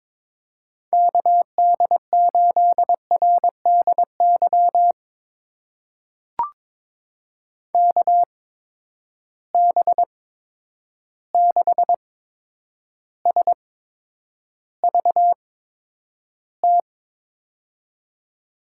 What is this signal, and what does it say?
Morse code, heard as a keyed tone, about 22 wpm, spaced as for 9 wpm, 700 Hz. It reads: KD8RDY KB6SVT
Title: USA Call Signs - 6 Characters - 8x Character Spacing - 22wpm